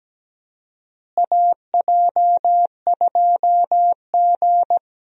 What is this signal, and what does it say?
Morse code, heard as a keyed tone, about 17 wpm, 700 Hz. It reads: AJ2G
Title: USA Call Signs - 4 Characters - Repeated 2x - 17wpm